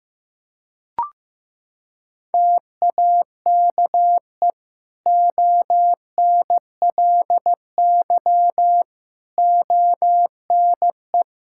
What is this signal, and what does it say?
Morse code, heard as a keyed tone, about 15 wpm, 700 Hz. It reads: TAKE ONLY ONE
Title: Sentences from Top 1000 Words - Review Part 5 - 15wpm